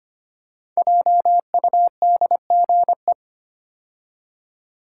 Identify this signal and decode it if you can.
Morse code, heard as a keyed tone, about 25 wpm, 700 Hz. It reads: JUDGE
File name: 5 Letter Words - First 3 Letters Given - 25wpm